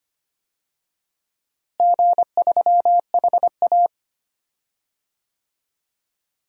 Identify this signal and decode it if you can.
Morse code, heard as a keyed tone, about 25 wpm, 700 Hz. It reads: G3HA
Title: DXCC-100 Call Signs - 4 Characters - Repeated 3x - 25wpm